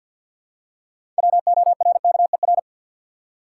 Morse code, heard as a keyed tone, about 50 wpm, 700 Hz. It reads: WORKER
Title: Six Letter Words - 50wpm